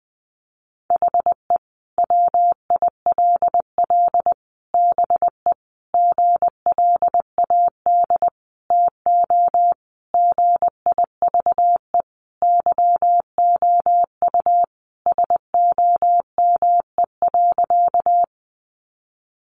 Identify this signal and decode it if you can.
Morse code, heard as a keyed tone, about 20 wpm, 700 Hz. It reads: HE WILL BE GLAD TO GIVE YOU SOME.
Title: Sentences from Top 800 Words 20wpm